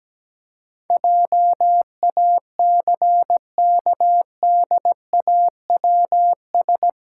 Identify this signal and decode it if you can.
Morse code, heard as a keyed tone, about 17 wpm, 700 Hz. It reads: JACKDAWS
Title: Pangram Challenge v2 - 17wpm